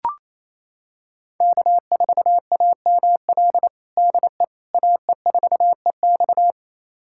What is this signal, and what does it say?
Morse code, heard as a keyed tone, about 28 wpm, 700 Hz. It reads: K4AML DE AE4EX
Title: USA Callsign DE Callsign - 5 Characters - 28wpm